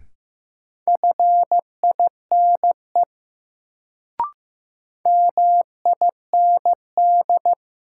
Morse code, heard as a keyed tone, about 15 wpm, 700 Hz. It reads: FINE MIND
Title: Sentences from Top 500 Words - New Word Drill - 15wpm